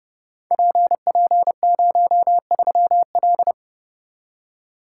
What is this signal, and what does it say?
Morse code, heard as a keyed tone, about 29 wpm, 700 Hz. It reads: PP03L
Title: Code Groups - 5 Letter:Numbers - 30wpm